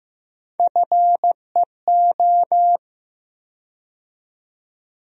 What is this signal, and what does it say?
Morse code, heard as a keyed tone, about 15 wpm, 700 Hz. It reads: FEO
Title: USA Call Sign Postfixes - 3 Letters 15wpm